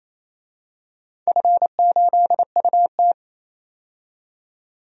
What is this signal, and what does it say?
Morse code, heard as a keyed tone, about 28 wpm, 700 Hz. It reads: F8UT
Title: DXCC-100 Call Signs - 4 Characters - 28wpm